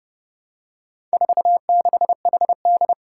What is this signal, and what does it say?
Morse code, heard as a keyed tone, about 30 wpm, 700 Hz. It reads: 46HD
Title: Four Challenging Letter-Number Combinations 30wpm